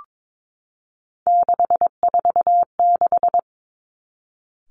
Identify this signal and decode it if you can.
Morse code, heard as a keyed tone, about 22 wpm, 700 Hz. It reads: 646
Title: Three Challenging Letter-Number Combinations 22wpm